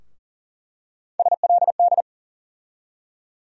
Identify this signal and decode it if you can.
Morse code, heard as a keyed tone, about 40 wpm, 700 Hz. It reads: SLD
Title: USA Call Sign Postfixes - 3 Letters 40wpm